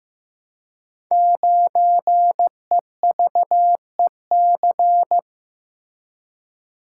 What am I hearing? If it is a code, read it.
Morse code, heard as a keyed tone, about 15 wpm, 700 Hz. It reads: 9EVEC
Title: Code Groups - 5 Letter:Numbers - 15wpm